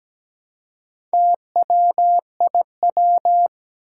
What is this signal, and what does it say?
Morse code, heard as a keyed tone, about 17 wpm, 700 Hz. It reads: TWIW
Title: Four Challenging Letter-Number Combinations 17wpm